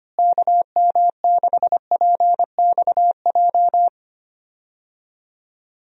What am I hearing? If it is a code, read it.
Morse code, heard as a keyed tone, about 25 wpm, 700 Hz. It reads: KM6PXJ
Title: USA Call Signs - 6 Characters - Repeated 3x - 25wpm